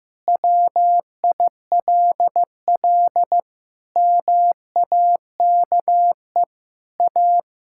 Morse code, heard as a keyed tone, about 16 wpm, 700 Hz. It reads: WILL MAKE A
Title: Pangram Challenge v2 - 15wpm